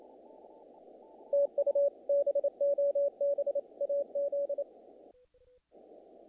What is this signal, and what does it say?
Morse code, heard as a keyed tone, about 29 wpm, 500 Hz. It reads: TU BOB AZ
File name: contests / naqp-cw-k5tr-op-k5pi- -1800